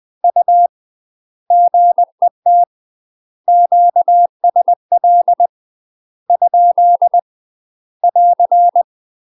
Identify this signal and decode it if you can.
Morse code, heard as a keyed tone, about 20 wpm, 700 Hz. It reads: U GET QSL ? <AR>
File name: UK MASTER QSOs TRAINING / 20WPM / UK MASTER QSOs 0050